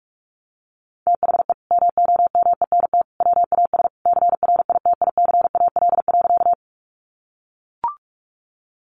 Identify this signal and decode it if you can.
Morse code, heard as a keyed tone, about 44 wpm, 700 Hz. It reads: THE MOMENT WAS CRITICAL.
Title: Sentences from Top 1100 Words 45wpm